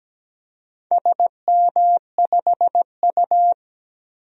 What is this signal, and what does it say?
Morse code, heard as a keyed tone, about 17 wpm, 700 Hz. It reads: SM5U
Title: DXCC-100 Call Signs - 4 Characters - Repeated 3x - 17wpm